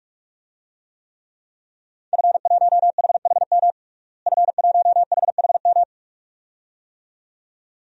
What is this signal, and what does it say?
Morse code, heard as a keyed tone, about 43 wpm, 700 Hz. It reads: F1HHM F1HHM
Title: DXCC-100 Call Signs - 5 Characters - Repeated 2x - 45wpm